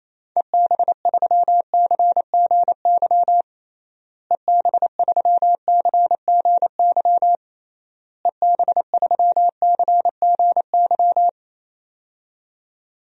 Morse code, heard as a keyed tone, about 28 wpm, 700 Hz. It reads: EB3CGY EB3CGY EB3CGY
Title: DXCC-100 Call Signs - 6 Characters - Repeated 3x - 28wpm